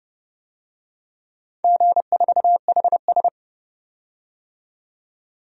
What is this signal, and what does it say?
Morse code, heard as a keyed tone, about 28 wpm, 700 Hz. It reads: G4HS
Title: DXCC-100 Call Signs - 4 Characters - 30wpm